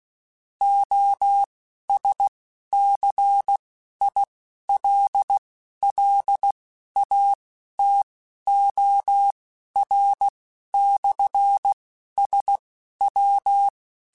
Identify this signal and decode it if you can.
Morse code, heard as a keyed tone, about 16 wpm, spaced as for 11 wpm, 800 Hz. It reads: OSCILLATOR/SW
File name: Technical Subjects / Technical Subjects-mp3s / Sw Pwr Sup @10